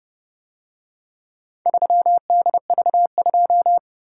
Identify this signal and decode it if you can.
Morse code, heard as a keyed tone, about 28 wpm, 700 Hz. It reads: 3DV2